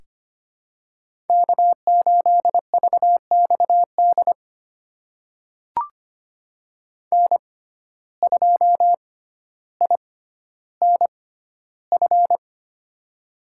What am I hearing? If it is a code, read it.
Morse code, heard as a keyed tone, about 25 wpm, spaced as for 13 wpm, 700 Hz. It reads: K8VXD N2INF
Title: USA Call Signs - 5 Characters - 6x Character Spacing - 25wpm